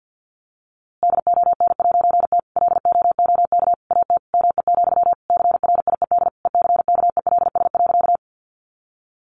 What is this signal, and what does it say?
Morse code, heard as a keyed tone, about 50 wpm, 700 Hz. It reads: DON'T LOOK AT ME, CRIED ECKELS.
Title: A Sound Of Thunder 50wpm